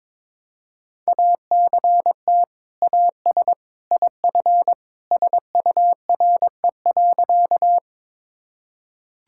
Morse code, heard as a keyed tone, about 22 wpm, 700 Hz. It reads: ACT AS IF SURE.
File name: practice / 22wpm / Sentences with 2-Word Phrases - 22wpm